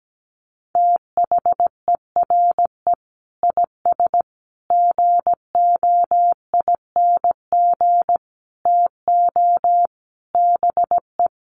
Morse code, heard as a keyed tone, about 17 wpm, 700 Hz. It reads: THERE IS GOING TO BE
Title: Sets of 5 Words Encore 17wpm